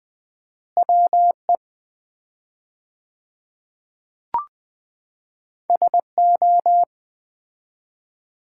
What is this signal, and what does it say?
Morse code, heard as a keyed tone, about 20 wpm, 700 Hz. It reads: WE SO
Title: 4x Word-Spacing - Sets of 5 Words - 20wpm